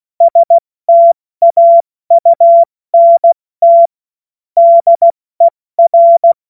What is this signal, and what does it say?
Morse code, heard as a keyed tone, about 16 wpm, 680 Hz. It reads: STAUNT DER
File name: Morsekurs / Witze / Witz 24 Juni Mai 2025 70 F80